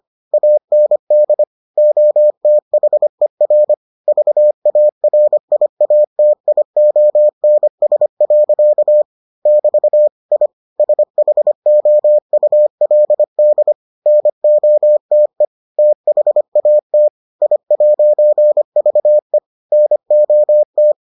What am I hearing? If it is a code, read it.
Morse code, heard as a keyed tone, about 25 wpm, 600 Hz. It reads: AND OTHER VARIATIONS. <BT> I SHOULD NOTE THAT I'VE NOT